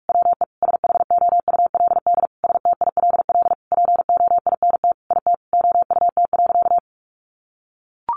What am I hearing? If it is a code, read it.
Morse code, heard as a keyed tone, about 44 wpm, 700 Hz. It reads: WE SHOULD STILL POINT IT OUT.
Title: Sentences from Top 1500 Words 45wpm